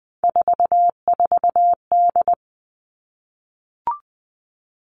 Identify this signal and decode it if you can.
Morse code, heard as a keyed tone, about 20 wpm, 700 Hz. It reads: 44D
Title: All Cut Numbers - 3-Digit 20wpm